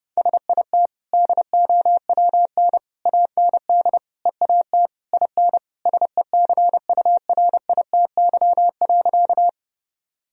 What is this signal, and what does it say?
Morse code, heard as a keyed tone, about 32 wpm, 700 Hz. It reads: SIT DOWN AND EAT IN SECURITY.
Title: Sentences from Top 1000 Words - Review Part 4 - 30wpm